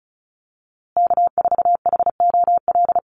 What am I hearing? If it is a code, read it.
Morse code, heard as a keyed tone, about 34 wpm, 700 Hz. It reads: K4HOL
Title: USA Call Signs Repeated 3x 35wpm